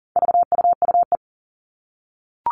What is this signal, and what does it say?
Morse code, heard as a keyed tone, about 40 wpm, 700 Hz. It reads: VUUE
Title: All Cut Numbers - 4-Digit 40wpm